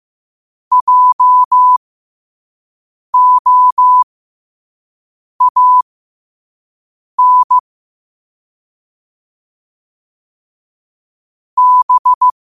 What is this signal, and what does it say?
Morse code, heard as a keyed tone, about 15 wpm, spaced as for 5 wpm, 1000 Hz. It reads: JOAN B
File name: qotdfiles / 5wpm